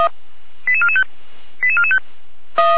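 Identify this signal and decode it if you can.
Five-tone selective call (ZVEI), heard as quick runs of five tones, tones of about 80 ms. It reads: 88445 88445
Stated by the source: Five-tone ZVEI selective call received by SDR, from feuerwehr-orlamuende.org fire brigade audio archive.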